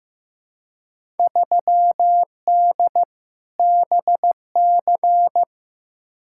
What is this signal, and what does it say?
Morse code, heard as a keyed tone, about 15 wpm, 700 Hz. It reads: 3D BC
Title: Field Day Exchanges Encore - 15wpm